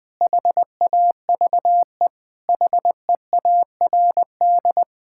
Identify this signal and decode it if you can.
Morse code, heard as a keyed tone, about 20 wpm, 700 Hz. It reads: HAVE HEARD
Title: Sentences with 8-Word Phrases - 20wpm